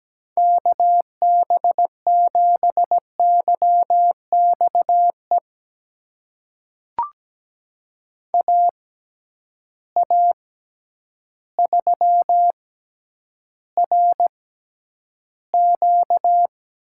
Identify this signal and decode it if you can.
Morse code, heard as a keyed tone, about 17 wpm, spaced as for 9 wpm, 700 Hz. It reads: KB7YXE AA3RQ
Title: USA Call Signs - 6x Character Spacing - 17wpm